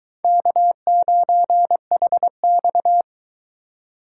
Morse code, heard as a keyed tone, about 22 wpm, 700 Hz. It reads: K9HX